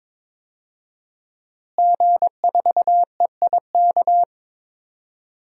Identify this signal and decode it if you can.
Morse code, heard as a keyed tone, about 22 wpm, 700 Hz. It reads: G4EIK
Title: DXCC-100 Call Signs - 5 Characters - Repeated 2x - 22wpm